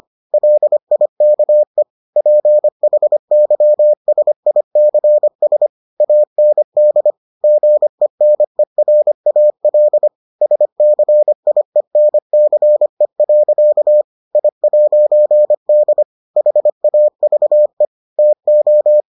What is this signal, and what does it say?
Morse code, heard as a keyed tone, about 25 wpm, 600 Hz. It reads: LIKE PHYSICS AND GENERAL SCIENCE. I'D HAVE TO